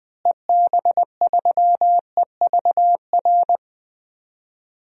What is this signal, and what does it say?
Morse code, heard as a keyed tone, about 21 wpm, 700 Hz. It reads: EB3EVR
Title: DXCC-100 Call Signs - 6 Characters - Repeated 3x - 20wpm